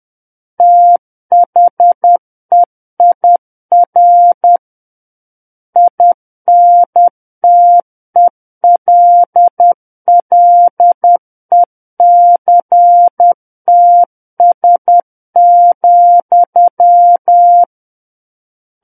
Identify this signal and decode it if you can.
Morse code, heard as a keyed tone, about 10 wpm, 700 Hz. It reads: THEIR INTELLECTS,
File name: Book One / 01-Chapter 1 - 10 WPM